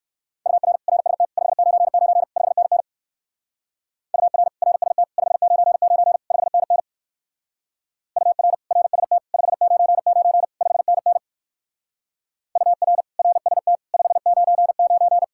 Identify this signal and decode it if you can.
Morse code, heard as a keyed tone, about 56 wpm, 700 Hz. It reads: UR RST 599 5NN UR RST 599 5NN UR RST 599 5NN UR RST 599
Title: Speed-Racing Common QSO Phrases - 45wpm